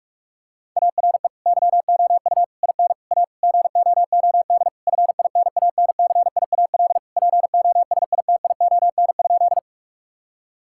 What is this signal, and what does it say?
Morse code, heard as a keyed tone, about 41 wpm, 700 Hz. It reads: ARE YOU IN A GOOD FINANCIAL POSITION?
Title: Sentences from Top 1000 Words - Review Part 5 - 45wpm